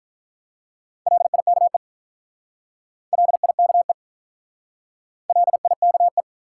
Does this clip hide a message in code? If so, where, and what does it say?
Yes: Morse code from 1.1 to 6.2 s, heard as a keyed tone, about 44 wpm, 700 Hz. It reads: LIKE LIKE LIKE